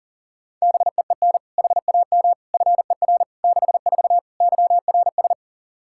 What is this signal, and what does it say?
Morse code, heard as a keyed tone, about 37 wpm, 700 Hz. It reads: BEEN HAM FER 64 YRS
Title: K1USN SST QSO Practice 40wpm